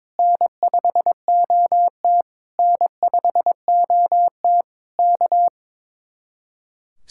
Morse code, heard as a keyed tone, about 22 wpm, 700 Hz. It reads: N5OT N5OT K
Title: Speed-Racing Common QSO Phrases Encore - 22wpm